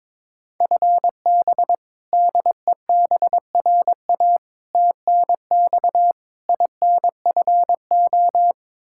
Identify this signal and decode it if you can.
Morse code, heard as a keyed tone, about 21 wpm, 700 Hz. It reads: FB DEBRA TNX INFO
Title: Practice QSO - Exchange 3 - More Info - 22wpm